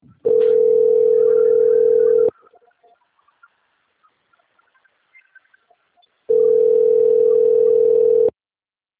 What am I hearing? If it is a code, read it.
Ringback tone (North America), 440 + 480 Hz.
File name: out- - - - - .820